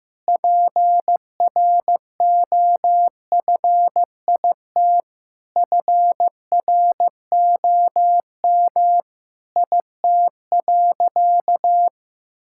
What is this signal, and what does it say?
Morse code, heard as a keyed tone, about 15 wpm, 700 Hz. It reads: PROFIT FROM IT.
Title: Sentences from Top 1000 Words - Review Part 5 - 15wpm